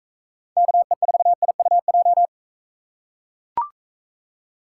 Morse code, heard as a keyed tone, about 40 wpm, 700 Hz. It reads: KE4IUJ